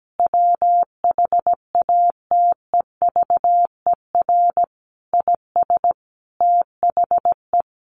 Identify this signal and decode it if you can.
Morse code, heard as a keyed tone, about 17 wpm, 700 Hz. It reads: WHATEVER IS THE
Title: Sentences from Top 1100 Words 17wpm